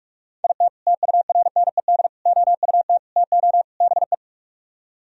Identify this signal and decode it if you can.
Morse code, heard as a keyed tone, about 45 wpm, 700 Hz. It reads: IT TURNED OUT TO BE